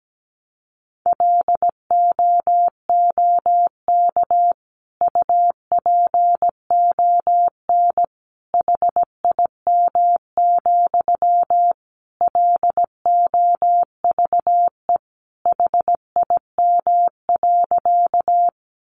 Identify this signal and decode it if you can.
Morse code, heard as a keyed tone, about 17 wpm, 700 Hz. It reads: LOOK UPON HIM, LOVE HIM.
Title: Sentences from Top 1100 Words 17wpm